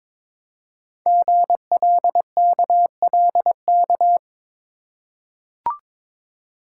Morse code, heard as a keyed tone, about 22 wpm, 700 Hz. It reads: GLKLK
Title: Code Groups - 5 Letters - 22wpm